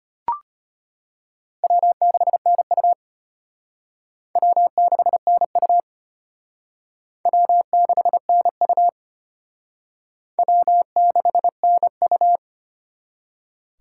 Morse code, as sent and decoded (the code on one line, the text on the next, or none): .-- -.... -. ..- / .-- -.... -. ..- / .-- -.... -. ..- / .-- -.... -. ..-
W6NU W6NU W6NU W6NU